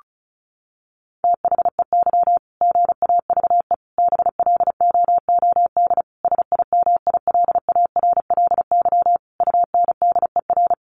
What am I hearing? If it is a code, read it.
Morse code, heard as a keyed tone, about 34 wpm, 700 Hz. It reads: THEY GAVE BLOOD SIMILARLY UNDER
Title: Sentences from Top 1400 Words 35wpm